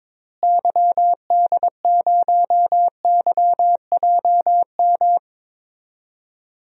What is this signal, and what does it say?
Morse code, heard as a keyed tone, about 22 wpm, 700 Hz. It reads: YD0YJM
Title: DXCC-100 Call Signs - 6 Characters - Repeated 3x - 22wpm